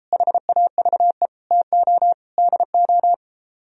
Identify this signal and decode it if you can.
Morse code, heard as a keyed tone, about 33 wpm, 700 Hz. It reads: HAVE TO DO